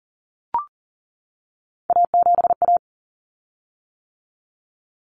Morse code, heard as a keyed tone, about 40 wpm, 700 Hz. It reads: A7A
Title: Common Cut Numbers - 3-Digit 40wpm